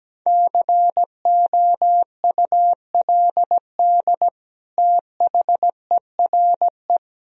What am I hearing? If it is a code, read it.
Morse code, heard as a keyed tone, about 17 wpm, 700 Hz. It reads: COULD THERE
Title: Sentences from Top 1000 Words - Review Part 4 - 17wpm